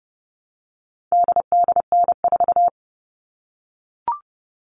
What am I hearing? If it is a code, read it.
Morse code, heard as a keyed tone, about 30 wpm, 700 Hz. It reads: DDN4
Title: All Cut Numbers - 4-Digit 30wpm